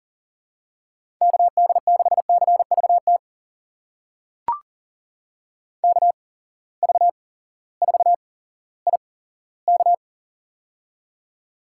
Morse code, heard as a keyed tone, about 40 wpm, spaced as for 18 wpm, 700 Hz. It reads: KD6CVT KV4IK
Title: USA Call Signs - 8x Character Spacing - 40wpm